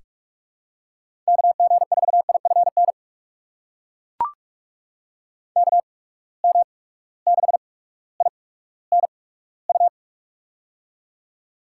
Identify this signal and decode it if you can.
Morse code, heard as a keyed tone, about 41 wpm, spaced as for 19 wpm, 700 Hz. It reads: KG4IFN KM6INU
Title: USA Call Signs - 6 Characters - 8x Character Spacing - 45wpm